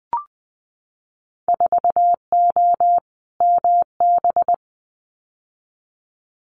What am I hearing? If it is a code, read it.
Morse code, heard as a keyed tone, about 20 wpm, 700 Hz. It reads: 4O MB
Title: Winter Field Day Exchanges Encore 20wpm